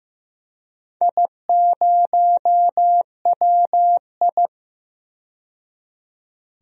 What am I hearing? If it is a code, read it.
Morse code, heard as a keyed tone, about 15 wpm, 700 Hz. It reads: I0WI
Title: DXCC-100 Call Signs - 4 Characters - 15wpm